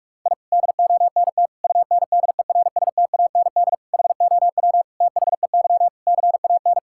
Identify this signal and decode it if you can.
Morse code, heard as a keyed tone, about 46 wpm, 700 Hz. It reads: I DONT UNDERSTAND HOW THEY CAN
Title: Sentences from Top 1000 Words - Review Part 5 - 45wpm